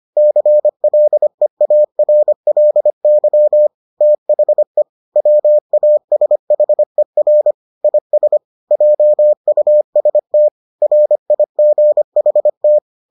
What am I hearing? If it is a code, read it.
Morse code, heard as a keyed tone, about 25 wpm, 600 Hz. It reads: CLEARLY THE WASHER IS JUST RIGHT